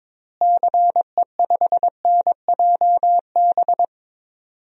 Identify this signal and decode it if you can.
Morse code, heard as a keyed tone, about 22 wpm, 700 Hz. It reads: CE5NJB
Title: DXCC-100 Call Signs - 6 Characters - Repeated 3x - 22wpm